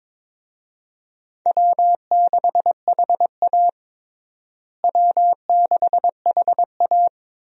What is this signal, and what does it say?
Morse code, heard as a keyed tone, about 22 wpm, 700 Hz. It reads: W6HA W6HA
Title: USA Call Signs - 4 Characters - Repeated 3x - 22wpm